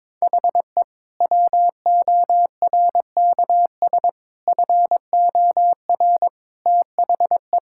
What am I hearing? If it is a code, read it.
Morse code, heard as a keyed tone, about 22 wpm, 700 Hz. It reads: HE WORKS FOR THE
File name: practice / 22wpm / Sentences with 2-Word Phrases - 22wpm